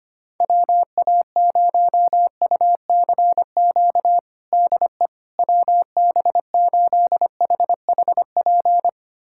Morse code, heard as a keyed tone, about 25 wpm, 700 Hz. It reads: WA0UCQ DE WB8HHP
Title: USA Callsign DE Callsign - 6 Characters - 25wpm